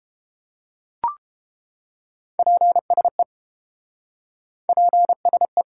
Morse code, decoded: PSE PSE

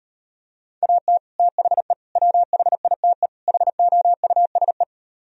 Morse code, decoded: AT THE WHITE HOUSE

